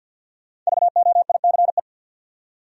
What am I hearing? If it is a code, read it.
Morse code, heard as a keyed tone, about 65 wpm, 700 Hz. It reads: VOICE